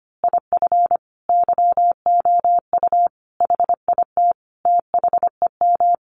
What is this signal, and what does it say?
Morse code, heard as a keyed tone, about 25 wpm, 700 Hz. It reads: IF YOU HIT THEM
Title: A Sound Of Thunder 25wpm